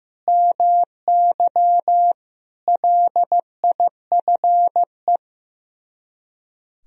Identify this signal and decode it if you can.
Morse code, heard as a keyed tone, about 15 wpm, 690 Hz. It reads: MY LIFE
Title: Speed-Racing - Sets of 5 Words 15wpm